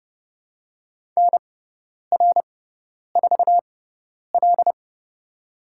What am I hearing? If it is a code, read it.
Morse code, heard as a keyed tone, about 30 wpm, spaced as for 10 wpm, 700 Hz. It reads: NR4L
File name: course / 30:10wpm / 102 Lesson 17g - Letter u - Call Signs 30wpm